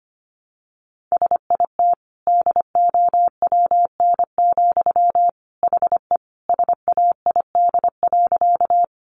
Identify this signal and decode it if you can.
Morse code, heard as a keyed tone, about 25 wpm, 700 Hz. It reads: SIT DOWN, HE SAID.
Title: Sentences from Top 500 Words 25wpm